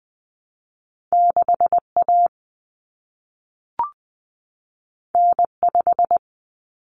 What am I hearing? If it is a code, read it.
Morse code, heard as a keyed tone, about 20 wpm, 700 Hz. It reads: 6A N5
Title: Common Cut Numbers - 2-Digit 20wpm